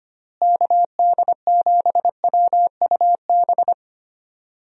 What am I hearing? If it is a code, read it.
Morse code, heard as a keyed tone, about 25 wpm, 700 Hz. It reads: KD7WUB
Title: USA Call Signs - 6 Characters - Repeated 3x - 25wpm